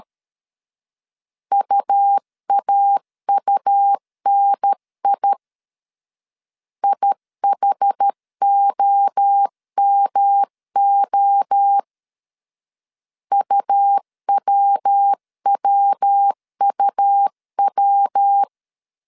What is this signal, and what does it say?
Morse code, heard as a keyed tone, about 13 wpm, 800 Hz. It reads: UAUNI IHOMO UWWUW
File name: CW / DET / 1 3 1500C S A 12wpm 2905